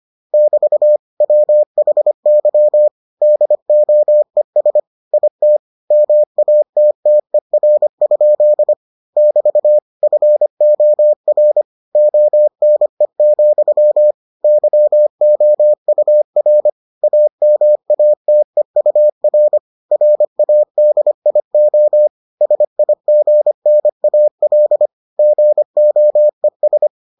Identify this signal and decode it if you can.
Morse code, heard as a keyed tone, about 25 wpm, 600 Hz. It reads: <BT> WHY DOES IT MATTER? <BT> FOR ONE, YOUR AMATEUR RADIO SIGNAL GOES